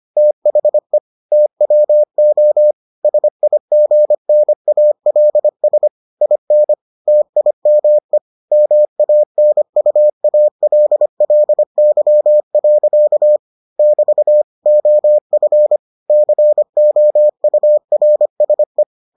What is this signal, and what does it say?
Morse code, heard as a keyed tone, about 25 wpm, 600 Hz. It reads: THE TWO SIGNALS IN TIME MANUALLY. <BT> OF COURSE